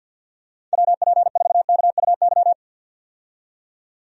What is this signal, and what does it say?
Morse code, heard as a keyed tone, about 52 wpm, 700 Hz. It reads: WP4KUY
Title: USA Call Signs - 6 Characters - 50wpm